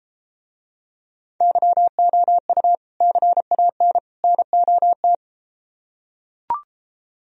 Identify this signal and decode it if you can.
Morse code, heard as a keyed tone, about 33 wpm, 700 Hz. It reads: YOU CAN NOT